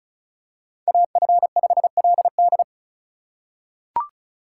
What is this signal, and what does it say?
Morse code, heard as a keyed tone, about 36 wpm, 700 Hz. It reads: AF5LD
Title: USA Call Signs - 2x Character Spacing - 35wpm